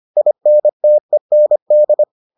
Morse code, heard as a keyed tone, about 25 wpm, 600 Hz. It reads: INTEND